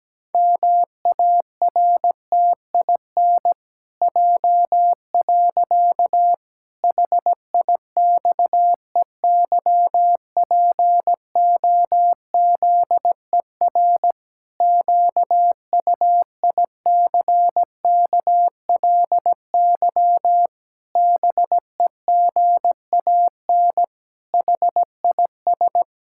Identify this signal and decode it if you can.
Morse code, heard as a keyed tone, about 17 wpm, 700 Hz. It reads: MARTIN J. HIXEYPOZER QUICKLY BEGAN HIS
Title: Pangram Challenge v2 - 17wpm